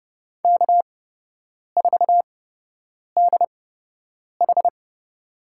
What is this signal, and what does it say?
Morse code, heard as a keyed tone, about 31 wpm, spaced as for 8 wpm, 700 Hz. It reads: K4DH
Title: USA Call Signs - 4 Characters - 8x Character Spacing - 30wpm